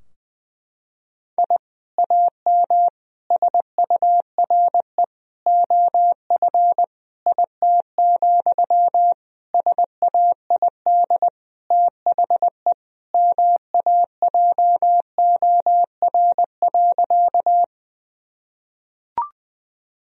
Morse code, heard as a keyed tone, about 20 wpm, 700 Hz. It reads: I AM SURE OF IT, SAID THE MAJOR.
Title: Sentences from Top 1000 Words - Review Part 4 - 20wpm